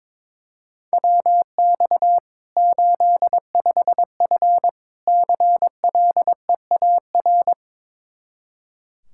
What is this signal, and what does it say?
Morse code, heard as a keyed tone, about 22 wpm, 700 Hz. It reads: WX 85F CLEAR
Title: Practice QSO - Exchange 2 - Info - 22wpm